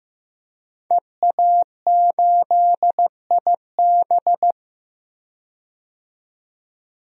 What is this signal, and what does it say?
Morse code, heard as a keyed tone, about 15 wpm, 700 Hz. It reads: EA8IB
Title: DXCC-100 Call Signs - 5 Characters - 15wpm